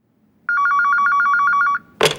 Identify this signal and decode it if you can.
There is no code.